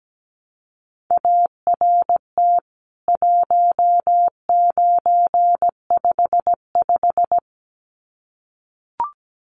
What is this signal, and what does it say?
Morse code, heard as a keyed tone, about 17 wpm, 700 Hz. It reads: ART 1955
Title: CWT Exchange 17wpm